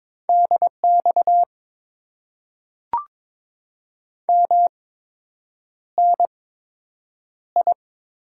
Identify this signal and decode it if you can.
Morse code, heard as a keyed tone, about 22 wpm, spaced as for 8 wpm, 700 Hz. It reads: DX MNI
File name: Top 90 QSO Elements - 8x Character Spacing - 22wpm